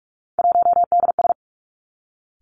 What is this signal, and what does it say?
Morse code, heard as a keyed tone, about 46 wpm, 700 Hz. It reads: 1DS
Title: Three Challenging Letter-Number Combinations 45wpm